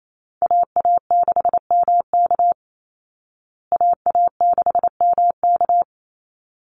Morse code, heard as a keyed tone, about 28 wpm, 700 Hz. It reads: AA6MK AA6MK
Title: Call Signs Repeated 3x - 28wpm